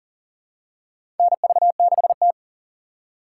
Morse code, heard as a keyed tone, about 40 wpm, 700 Hz. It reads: NV6T
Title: USA Call Signs - 4 Characters - 6x Character Spacing - 40wpm